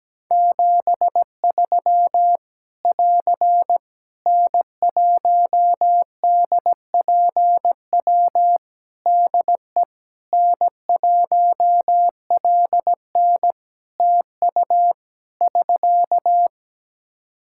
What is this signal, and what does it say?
Morse code, heard as a keyed tone, about 17 wpm, 700 Hz. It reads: 73 <AR> N1DPW DE N1LN TU <SK>
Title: Practice QSO - Exchange 4 - Ending - 17wpm